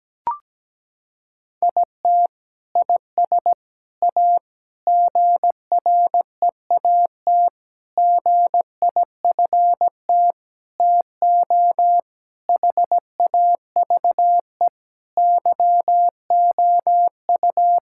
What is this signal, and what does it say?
Morse code, heard as a keyed tone, about 17 wpm, 700 Hz. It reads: IT IS A GREAT GIFT TO HAVE YOU
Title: Sentences from Top 1000 Words - Review Part 6 - 17wpm